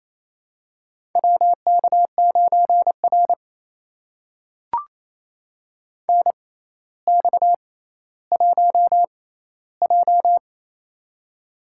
Morse code, heard as a keyed tone, about 27 wpm, spaced as for 14 wpm, 700 Hz. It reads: WK9R NX1J
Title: USA Call Signs - 4 Characters - 6x Character Spacing - 28wpm